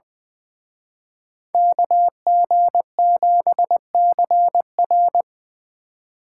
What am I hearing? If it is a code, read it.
Morse code, heard as a keyed tone, about 23 wpm, 700 Hz. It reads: KG7CR
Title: Speed Racing - USA 2x2 Call Signs 20wpm